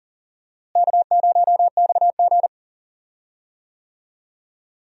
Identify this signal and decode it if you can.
Morse code, heard as a keyed tone, about 38 wpm, 700 Hz. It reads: K0XG
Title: USA Call Signs - 4 Characters - Repeated 3x - 40wpm